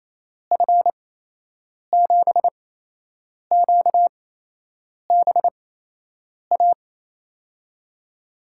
Morse code, heard as a keyed tone, about 28 wpm, spaced as for 8 wpm, 700 Hz. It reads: F7QBA
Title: USA Call Signs - 8x Character Spacing - 28wpm